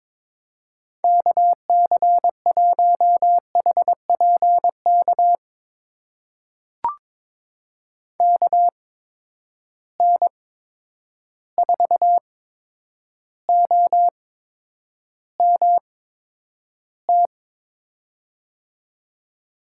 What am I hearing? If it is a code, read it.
Morse code, heard as a keyed tone, about 22 wpm, spaced as for 9 wpm, 700 Hz. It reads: KC1HPK KN4OMT
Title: USA Call Signs - 6 Characters - 8x Character Spacing - 22wpm